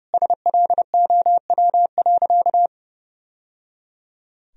Morse code, heard as a keyed tone, about 32 wpm, 700 Hz. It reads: SLOW.
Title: Sentences from Top 1000 Words - Review Part 2 - 30wpm